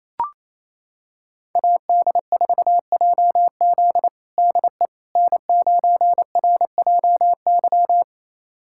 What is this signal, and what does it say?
Morse code, heard as a keyed tone, about 27 wpm, 700 Hz. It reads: AD4JZ DE N9RJY
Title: USA Callsign DE Callsign - 5 Characters - 28wpm